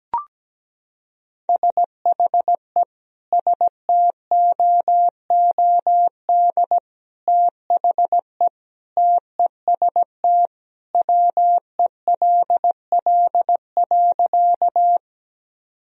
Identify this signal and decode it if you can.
Morse code, heard as a keyed tone, about 17 wpm, 700 Hz. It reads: SHE STOOD THE TEST WELL.
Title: Sentences from Top 1000 Words - Review Part 3 - 17wpm